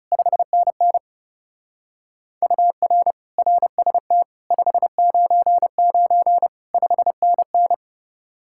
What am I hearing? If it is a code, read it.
Morse code, heard as a keyed tone, about 35 wpm, 700 Hz. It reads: 5NN UR RST 599 5NN
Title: Speed-Racing Common QSO Phrases Encore - 30wpm